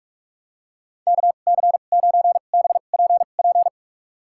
Morse code, heard as a keyed tone, about 46 wpm, spaced as for 32 wpm, 700 Hz. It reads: KC9BPP
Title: USA Call Signs - 6 Characters - 2x Character Spacing - 45wpm